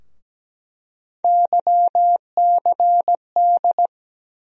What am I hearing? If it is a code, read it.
Morse code, heard as a keyed tone, about 17 wpm, 700 Hz. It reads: YCD
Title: USA Call Sign Postfixes - 3 Letters 17wpm